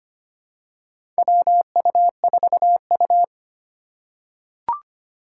Morse code, heard as a keyed tone, about 25 wpm, 700 Hz. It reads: WU4U
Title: USA Call Signs - 4 Characters - 4x Character Spacing - 25wpm